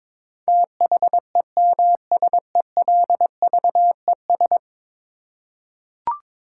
Morse code, heard as a keyed tone, about 21 wpm, 700 Hz. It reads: THEMSELVES
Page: morse-code-ninja-practice-22wpm